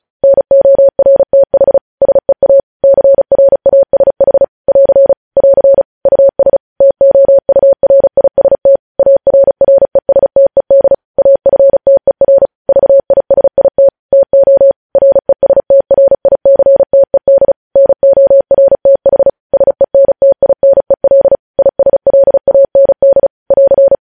Morse code, heard as a keyed tone, about 36 wpm, 550 Hz. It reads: NORTH SEA CRASH <AR> <AR> US TOURIST ARRESTED AFTER VISIT TO RESTRICTED NORTH SENTINEL ISLAND <AR>
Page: news headlines in morse code at 35 wpm